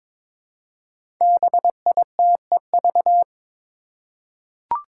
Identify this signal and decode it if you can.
Morse code, heard as a keyed tone, about 22 wpm, 700 Hz. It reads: BITEV
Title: Five Challenging Letter-Number Combinations 22wpm